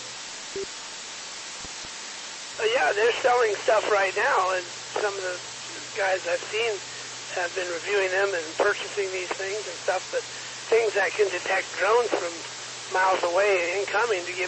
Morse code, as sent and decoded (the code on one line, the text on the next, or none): none